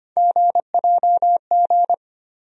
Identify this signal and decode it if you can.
Morse code, heard as a keyed tone, about 25 wpm, 700 Hz. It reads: GJG
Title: USA Call Sign Postfixes - 3 Letters 25wpm